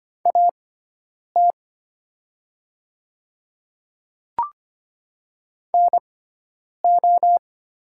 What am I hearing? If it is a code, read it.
Morse code, heard as a keyed tone, about 25 wpm, spaced as for 9 wpm, 700 Hz. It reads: AT NO
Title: 6x Character Spacing - Top 100 Words - 25wpm